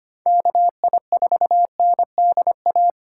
Morse code, heard as a keyed tone, about 25 wpm, 700 Hz. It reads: KI4NDA
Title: USA Call Signs - 6 Characters - 6x Character Spacing - 25wpm